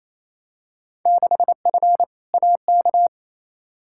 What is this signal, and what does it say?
Morse code, heard as a keyed tone, about 28 wpm, 700 Hz. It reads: 6F AK
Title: Field Day Exchanges Encore - 28wpm